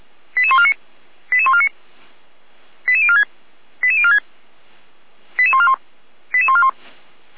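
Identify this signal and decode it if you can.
Five-tone selective call (ZVEI), heard as quick runs of five tones, tones of about 70 ms. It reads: 88149 88149 88046 88046 88141 88141
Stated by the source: ZVEI five-tone selective call received with an SDR.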